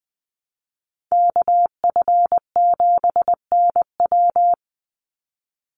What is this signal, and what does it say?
Morse code, heard as a keyed tone, about 20 wpm, 700 Hz. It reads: KF7NW
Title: USA Call Signs 20wpm